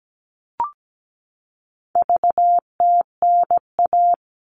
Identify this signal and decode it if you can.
Morse code, heard as a keyed tone, about 17 wpm, 700 Hz. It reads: VTNA
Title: All Cut Numbers - 4-Digit 17wpm